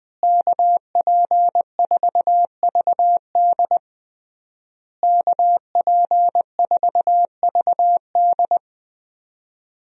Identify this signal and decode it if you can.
Morse code, heard as a keyed tone, about 20 wpm, 700 Hz. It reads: KP4VD KP4VD
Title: DXCC-100 Call Signs - 5 Characters - Repeated 3x - 20wpm